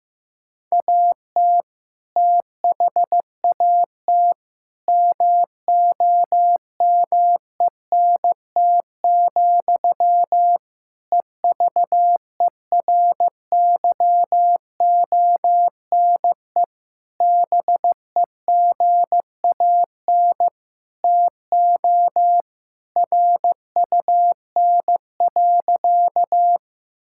Morse code, heard as a keyed tone, about 15 wpm, 700 Hz. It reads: AT THAT MOMENT, EVERYONE BEGAN TO RUN.
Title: Sentences from Top 1000 Words - Review Part 5 - 15wpm